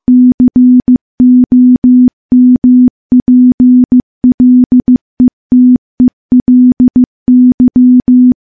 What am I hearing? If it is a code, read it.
Morse code, heard as a keyed tone, about 15 wpm, 250 Hz. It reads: COMPLETELY
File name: morse / morse-training-15WPM